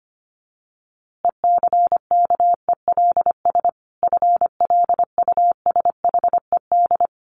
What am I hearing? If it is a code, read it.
Morse code, heard as a keyed tone, about 25 wpm, 700 Hz. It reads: ECKELS FLUSHED
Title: A Sound Of Thunder 25wpm